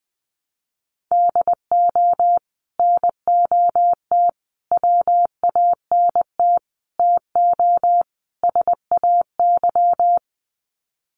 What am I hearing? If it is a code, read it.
Morse code, heard as a keyed tone, about 20 wpm, 700 Hz. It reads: DO NOT WANT TO SAY
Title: Sets of 5 Words Encore 20wpm